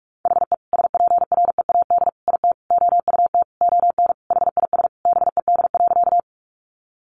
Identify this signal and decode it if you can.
Morse code, heard as a keyed tone, about 44 wpm, 700 Hz. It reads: HE SPREAD IT OUT ON HIS BED.
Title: Sentences from Top 1200 Words 45wpm